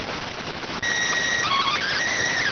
SSTV (slow-scan television), PD 290.